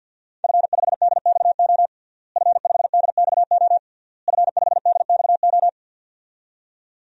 Morse code, heard as a keyed tone, about 50 wpm, 700 Hz. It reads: F5DXO F5DXO F5DXO
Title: DXCC-100 Call Signs - 5 Characters - Repeated 3x - 50wpm